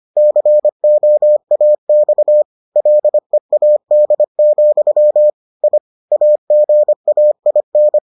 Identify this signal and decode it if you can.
Morse code, heard as a keyed tone, about 25 wpm, 600 Hz. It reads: COAX LEAD, I AGAIN